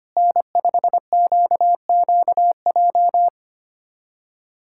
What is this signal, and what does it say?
Morse code, heard as a keyed tone, about 25 wpm, 700 Hz. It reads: N5QQJ
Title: USA Callsign DE Callsign - 5 Characters - 25wpm